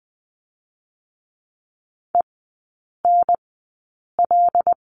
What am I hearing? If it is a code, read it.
Morse code, heard as a keyed tone, about 19 wpm, spaced as for 8 wpm, 700 Hz. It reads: ENL
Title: 184 Lesson 30e - Letter b - Three Characters 20wpm